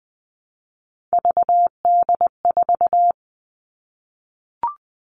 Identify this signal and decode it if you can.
Morse code, heard as a keyed tone, about 20 wpm, 700 Hz. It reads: VD4
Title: Three Challenging Letter-Number Combinations 20wpm